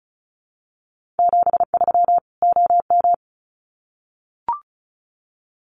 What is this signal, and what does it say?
Morse code, heard as a keyed tone, about 34 wpm, 700 Hz. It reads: 73 OM
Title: Common QSO Phrases 35wpm